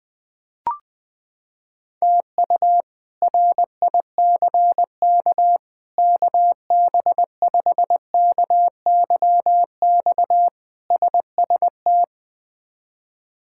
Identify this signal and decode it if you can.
Morse code, heard as a keyed tone, about 20 wpm, 700 Hz. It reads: TU RICK KB5KYX SST
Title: K1USN SST QSO Practice - 20wpm